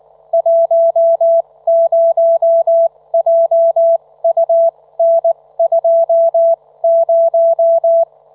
Morse code, heard as a keyed tone, about 19 wpm, spaced as for 16 wpm, 670 Hz. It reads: 10JUN20
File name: Trivia